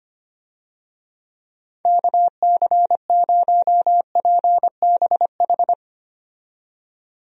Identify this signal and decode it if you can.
Morse code, heard as a keyed tone, about 25 wpm, 700 Hz. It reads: KC0PBH
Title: USA Call Signs - 6 Characters - 25wpm